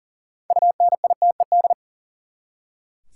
Morse code, heard as a keyed tone, about 40 wpm, 700 Hz. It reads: UNITED
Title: Six Letter Words - 40wpm